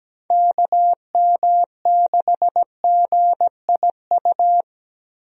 Morse code, heard as a keyed tone, about 17 wpm, 700 Hz. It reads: KM6GIU
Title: USA Call Signs - 6 Characters - Repeated 2x - 17wpm